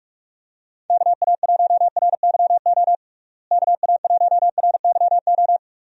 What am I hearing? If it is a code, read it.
Morse code, heard as a keyed tone, about 46 wpm, 700 Hz. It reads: KA1RYO KA1RYO
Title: USA Call Signs - 6 Characters - Repeated 3x - 45wpm